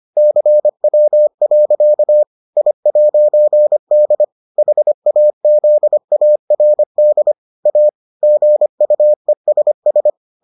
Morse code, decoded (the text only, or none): CW. I'D HAZARD A GUESS